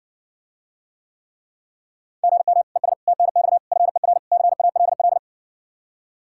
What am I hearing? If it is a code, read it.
Morse code, heard as a keyed tone, about 60 wpm, 700 Hz. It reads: GM ES TNX FER CALL